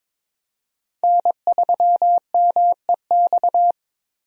Morse code, heard as a keyed tone, about 21 wpm, 700 Hz. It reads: N3MEX
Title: USA Call Signs - 5 Characters - 22wpm